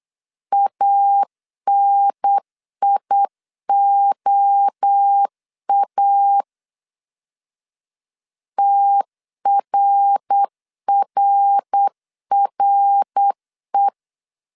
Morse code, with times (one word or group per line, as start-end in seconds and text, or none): 0.5-6.4 ANIOA
8.6-13.9 TRRRE